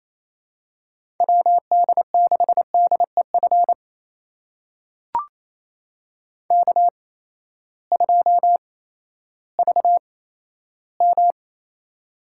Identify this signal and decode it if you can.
Morse code, heard as a keyed tone, about 28 wpm, spaced as for 14 wpm, 700 Hz. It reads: WD6DEF K2VM
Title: USA Call Signs - 8x Character Spacing - 28wpm